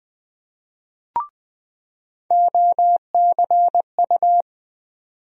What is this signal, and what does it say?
Morse code, heard as a keyed tone, about 20 wpm, 700 Hz. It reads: OCU
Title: USA Call Sign Postfixes - 3 Letters 20wpm